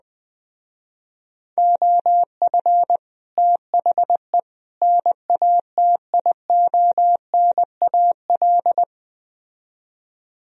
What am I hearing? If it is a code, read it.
Morse code, heard as a keyed tone, about 20 wpm, 700 Hz. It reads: OF THE NATIONAL